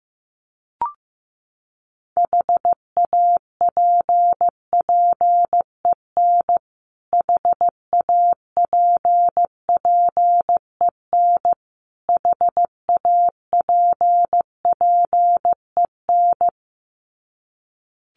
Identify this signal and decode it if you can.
Morse code, heard as a keyed tone, about 15 wpm, 700 Hz. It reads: HAPPEN HAPPEN HAPPEN
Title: Top 500 Words Repeated 15wpm